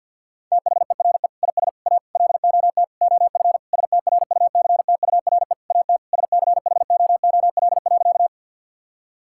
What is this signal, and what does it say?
Morse code, heard as a keyed tone, about 50 wpm, 700 Hz. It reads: THERE IS A LOT OF STRUCTURE AT SCHOOL.
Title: Sentences from Top 1000 Words - Review Part 6 - 50wpm